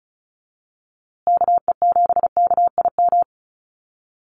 Morse code, heard as a keyed tone, about 34 wpm, 700 Hz. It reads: KE7KIM
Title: USA Call Signs Repeated 3x 35wpm